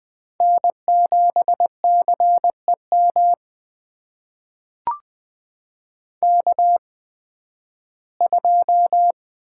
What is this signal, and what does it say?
Morse code, heard as a keyed tone, about 20 wpm, spaced as for 13 wpm, 700 Hz. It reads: N7CEM K2
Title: USA Call Signs - 5 Characters - 8x Character Spacing - 20wpm